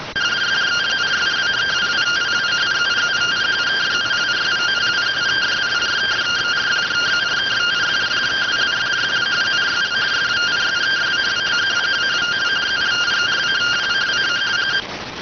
Two-tone teleprinter signal, about 170 Hz shift.